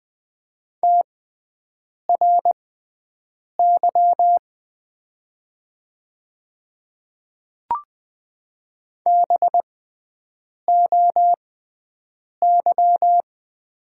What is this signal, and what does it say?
Morse code, heard as a keyed tone, about 20 wpm, spaced as for 7 wpm, 700 Hz. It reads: TRY BOY
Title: 6x Character Spacing - Top 200 Words - 20wpm